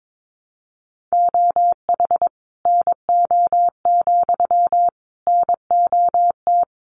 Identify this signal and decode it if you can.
Morse code, heard as a keyed tone, about 22 wpm, 700 Hz. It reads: OH NO, NOT
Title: Sentences from Top 1300 Words 22wpm